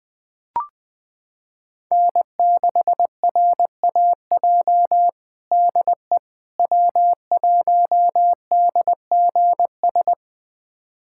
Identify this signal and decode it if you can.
Morse code, heard as a keyed tone, about 20 wpm, 700 Hz. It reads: N6RAJ DE W1DGS
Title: USA Callsign DE Callsign - 5 Characters - 20wpm